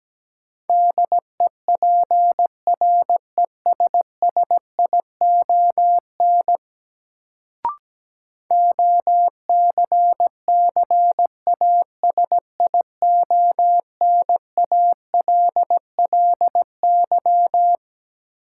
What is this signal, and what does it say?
Morse code, heard as a keyed tone, about 17 wpm, 700 Hz. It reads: DEPRESSION OCCASIONALLY
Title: Sentences from Top 1300 Words - New Word Drill - 17wpm